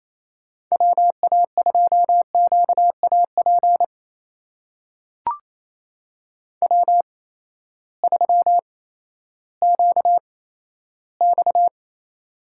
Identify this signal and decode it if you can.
Morse code, heard as a keyed tone, about 28 wpm, spaced as for 14 wpm, 700 Hz. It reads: WA2QAP W3QX
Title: USA Call Signs - 8x Character Spacing - 28wpm